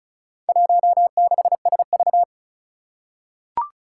Morse code, heard as a keyed tone, about 36 wpm, 700 Hz. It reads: 16SV
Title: Four Challenging Letter-Number Combinations 35wpm